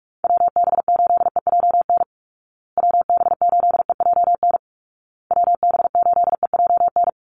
Morse code, heard as a keyed tone, about 44 wpm, 700 Hz. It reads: WB8EJN WB8EJN WB8EJN
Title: Call Signs Repeated 2x - 45wpm